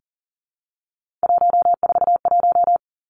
Morse code, heard as a keyed tone, about 38 wpm, 700 Hz. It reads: 141
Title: Three Challenging Letter-Number Combinations 40wpm